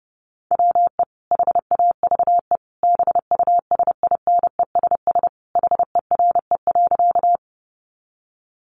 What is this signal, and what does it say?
Morse code, heard as a keyed tone, about 30 wpm, 700 Hz. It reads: WE HAVE BUSINESS HERE.
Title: Sentences from Top 600 Words 30wpm